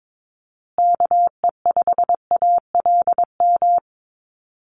Morse code, heard as a keyed tone, about 22 wpm, 700 Hz. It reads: KE5ALM
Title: USA Call Signs Repeated 3x 22wpm